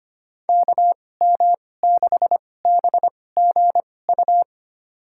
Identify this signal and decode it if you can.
Morse code, heard as a keyed tone, about 25 wpm, spaced as for 18 wpm, 690 Hz. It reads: KM6BGU
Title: USA Call Signs - 6 Characters - 2x Character Spacing - 25wpm